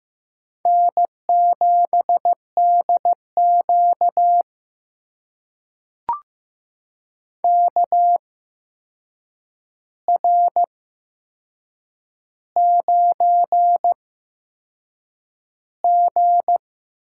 Morse code, heard as a keyed tone, about 15 wpm, spaced as for 6 wpm, 700 Hz. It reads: N7DQ KR9G
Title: USA Call Signs - 8x Character Spacing - 15wpm